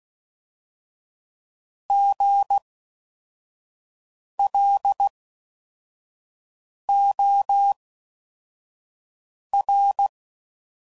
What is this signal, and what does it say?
Morse code, heard as a keyed tone, about 16 wpm, spaced as for 4 wpm, 800 Hz. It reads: GLOR